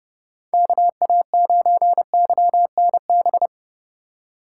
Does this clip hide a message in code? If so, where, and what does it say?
Yes: Morse code from 0.5 to 3.5 s, heard as a keyed tone, about 39 wpm, 700 Hz. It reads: KA9YNB